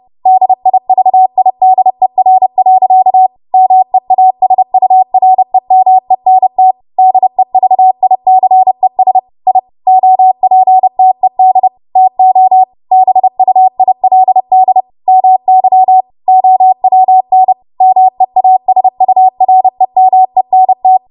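Morse code, heard as a keyed tone, about 30 wpm, 750 Hz. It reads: DIVIDER. MEASUREMENT DEVICES I OPTED TO BUILD MY OWN MEASUREMENT
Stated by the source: W1AW Web Code Practice Files, 30 WPM run from ARRL; text from April 2023 QST.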